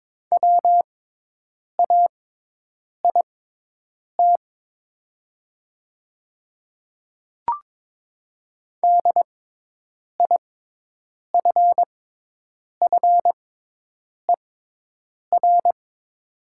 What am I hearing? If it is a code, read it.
Morse code, heard as a keyed tone, about 22 wpm, spaced as for 8 wpm, 700 Hz. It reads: WAIT DIFFER
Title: 6x Character Spacing - Top 500 Words - 22wpm